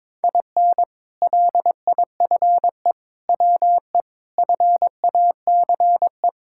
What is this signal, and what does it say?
Morse code, heard as a keyed tone, about 22 wpm, 700 Hz. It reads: IN LIFE WE FACE
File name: Sentences with 7-Word Phrases - 22wpm